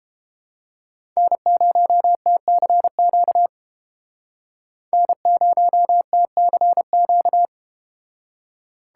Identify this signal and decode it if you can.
Morse code, heard as a keyed tone, about 32 wpm, 700 Hz. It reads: N0TCQ N0TCQ